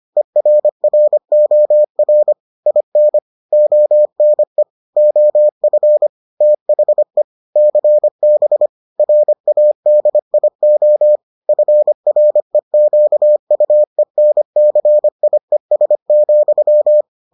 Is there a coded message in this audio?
Yes: Morse code from 0.2 to 17.0 s, heard as a keyed tone, about 25 wpm, 600 Hz. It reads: ERROR IN ONE OF THE CB RADIO FREQUENCIES,